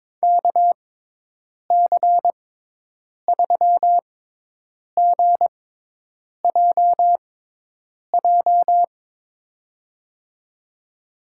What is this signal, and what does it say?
Morse code, heard as a keyed tone, about 23 wpm, spaced as for 8 wpm, 700 Hz. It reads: KC3GJJ
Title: USA Call Signs - 6 Characters - 6x Character Spacing - 22wpm